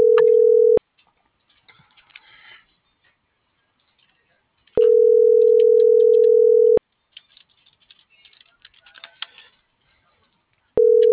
Ringback tone (North America), 440 + 480 Hz.